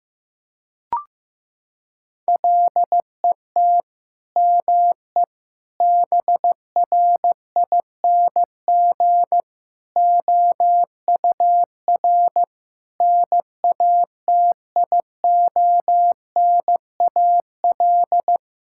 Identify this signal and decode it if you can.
Morse code, heard as a keyed tone, about 15 wpm, 700 Hz. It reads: LET ME BRING OUR NATIONAL
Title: Sentences from Top 1000 Words - Review Part 6 - 15wpm